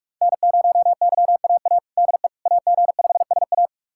Morse code, heard as a keyed tone, about 46 wpm, 700 Hz. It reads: N0YAA DE AG5SA
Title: USA Callsign DE Callsign 45wpm